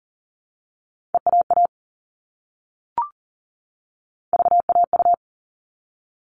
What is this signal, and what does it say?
Morse code, heard as a keyed tone, about 40 wpm, 700 Hz. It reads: EAA VAU
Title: All Cut Numbers - 3-Digit 40wpm